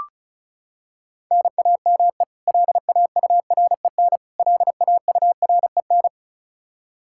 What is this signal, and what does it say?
Morse code, heard as a keyed tone, about 34 wpm, 700 Hz. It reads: NAME LAUREN LAUREN
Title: Practice QSO - Exchange 1 - Beginning - 35wpm